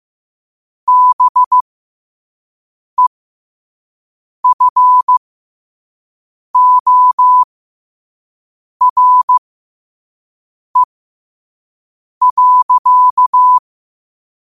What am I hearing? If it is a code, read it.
Morse code, heard as a keyed tone, about 15 wpm, spaced as for 5 wpm, 1000 Hz. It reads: BEFORE.
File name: qotdfiles / 5wpm